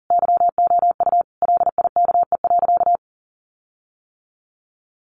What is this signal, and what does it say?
Morse code, heard as a keyed tone, about 40 wpm, 700 Hz. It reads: YOU LIKE.
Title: Sentences from Top 1500 Words 40wpm